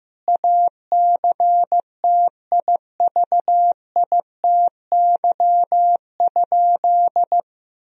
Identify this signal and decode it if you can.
Morse code, heard as a keyed tone, about 15 wpm, 700 Hz. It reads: ACTIVITY?
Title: Sentences from Top 1000 Words - Review Part 4 - 15wpm